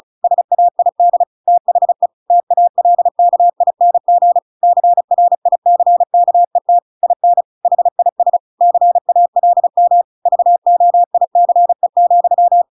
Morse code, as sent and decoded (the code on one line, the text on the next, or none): ... .- .. -.. / - .... / - .- .-.. -.- .. -. --. / -.-. .-. .. -.-. -.- . - / .. -. / .... .. ... / -.-. .- .-.. -- / ...- --- .. -.-. --..--
SAID THE TALKING CRICKET IN HIS CALM VOICE,